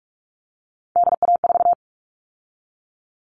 Morse code, heard as a keyed tone, about 44 wpm, 700 Hz. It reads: DA4
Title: All Cut Numbers - 3-Digit 45wpm